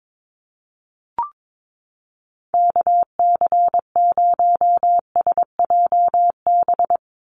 Morse code, heard as a keyed tone, about 22 wpm, 700 Hz. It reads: KC0SJB
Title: USA Call Signs 22wpm